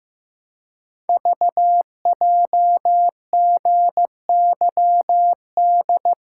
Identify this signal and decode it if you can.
Morse code, heard as a keyed tone, about 15 wpm, 700 Hz. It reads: VJGYD
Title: Code Groups - 5 Letters - 15wpm